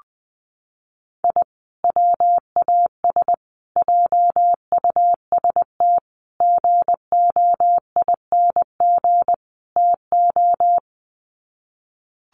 Morse code, heard as a keyed tone, about 20 wpm, 700 Hz. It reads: I WAS JUST GOING TO
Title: Sets of 5 Words Encore 20wpm